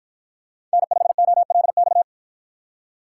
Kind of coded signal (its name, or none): Morse code